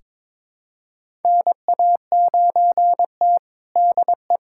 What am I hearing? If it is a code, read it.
Morse code, heard as a keyed tone, about 22 wpm, 700 Hz. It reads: NA9T DE
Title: USA Callsign DE Callsign - 4 Characters - 22wpm